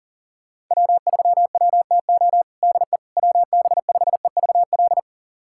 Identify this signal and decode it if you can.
Morse code, heard as a keyed tone, about 37 wpm, 700 Hz. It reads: W3WTO DE WB5EVL
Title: USA Callsign DE Callsign 40wpm